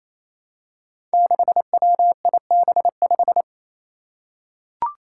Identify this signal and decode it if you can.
Morse code, heard as a keyed tone, about 28 wpm, 700 Hz. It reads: 6WIB5
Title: Five Challenging Letter-Number Combinations 28wpm